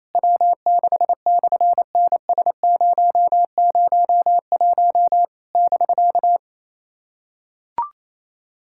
Morse code, heard as a keyed tone, about 27 wpm, 700 Hz. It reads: W6/NS001 <BK>
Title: SOTA QSO Practice 28wpm